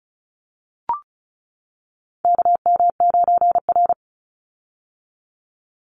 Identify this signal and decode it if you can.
Morse code, heard as a keyed tone, about 34 wpm, 700 Hz. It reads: KM9R
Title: Call Signs - 35wpm